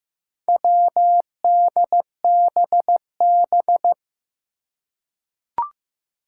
Morse code, heard as a keyed tone, about 15 wpm, 700 Hz. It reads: WDBB
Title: Four Challenging Letter-Number Combinations 15wpm